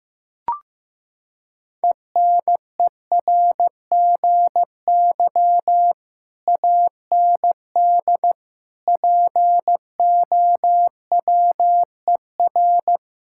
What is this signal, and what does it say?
Morse code, heard as a keyed tone, about 15 wpm, 700 Hz. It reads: ENERGY AND POWER
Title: Sentences from Top 1000 Words - Review Part 6 - 15wpm